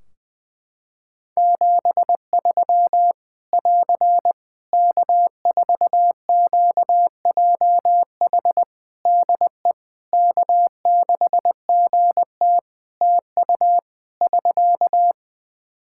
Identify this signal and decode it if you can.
Morse code, heard as a keyed tone, about 20 wpm, 700 Hz. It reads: 73 <AR> K4QJH DE K6GT TU <SK>
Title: Practice QSO - Exchange 4 - Ending - 20wpm